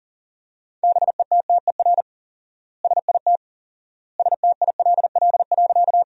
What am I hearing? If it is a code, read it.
Morse code, heard as a keyed tone, about 40 wpm, 700 Hz. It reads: BETTER SIT STILL.
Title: 4x Word-Spacing - Sentences from Top 500 Words - 40wpm